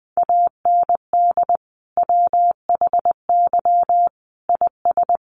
Morse code, heard as a keyed tone, about 20 wpm, 700 Hz. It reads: AND WHY IS
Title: Sentences from Top 200 Words 20wpm